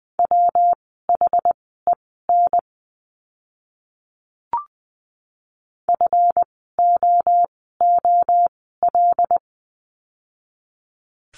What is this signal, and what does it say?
Morse code, heard as a keyed tone, about 20 wpm, spaced as for 14 wpm, 700 Hz. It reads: WHEN FOOL
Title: 139 Lesson 23f - Letter f - Words 20wpm